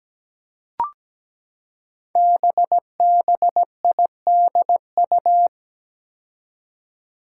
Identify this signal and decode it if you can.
Morse code, heard as a keyed tone, about 17 wpm, 700 Hz. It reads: BBIDU
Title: Five Challenging Letter-Number Combinations 17wpm